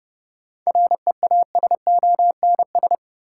Morse code, heard as a keyed tone, about 32 wpm, 700 Hz. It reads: REASONS
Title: Sets of 4 Words - New Word Drill - 30wpm